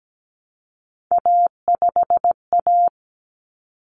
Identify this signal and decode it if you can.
Morse code, heard as a keyed tone, about 17 wpm, 700 Hz. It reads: A5A
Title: Common Cut Numbers - 3-Digit 17wpm